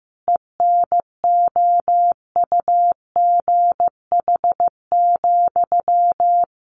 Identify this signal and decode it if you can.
Morse code, heard as a keyed tone, about 15 wpm, 700 Hz. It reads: ENOUGH,
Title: Sentences from Top 1400 Words 15wpm